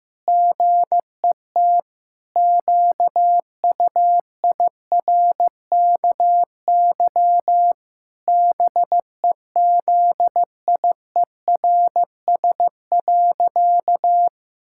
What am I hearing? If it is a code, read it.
Morse code, heard as a keyed tone, about 16 wpm, 700 Hz. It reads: GET QUIRKY BEZIERS.
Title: Pangram Challenge v2 - 15wpm